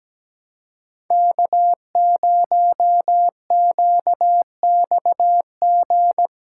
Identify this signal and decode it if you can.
Morse code, heard as a keyed tone, about 17 wpm, 700 Hz. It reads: K0QXG
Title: USA Call Signs - 5 Characters - 3x Character Spacing - 17wpm